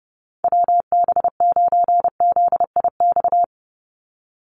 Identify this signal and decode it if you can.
Morse code, heard as a keyed tone, about 30 wpm, 700 Hz. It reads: WB9ZIX
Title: USA Call Signs 30wpm